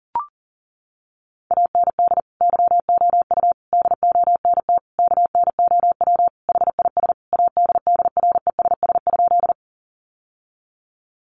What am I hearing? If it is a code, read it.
Morse code, heard as a keyed tone, about 40 wpm, 700 Hz. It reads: AND YOU DONT KNOW HIS ADDRESS?
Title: Sentences from Top 1000 Words 40wpm